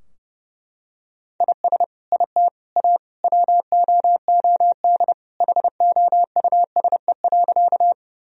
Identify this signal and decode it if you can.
Morse code, heard as a keyed tone, about 30 wpm, 700 Hz. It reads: IS IT A WOOD HOUSE.
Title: Sentences from Top 1000 Words - Review Part 2 - 30wpm